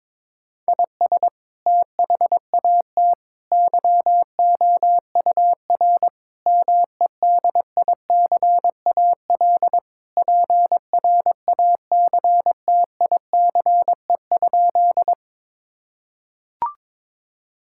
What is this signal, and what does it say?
Morse code, heard as a keyed tone, about 21 wpm, 700 Hz. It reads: IS THAT YOUR MEDICAL PRACTICE?
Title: Sentences from Top 1000 Words - Review Part 5 - 22wpm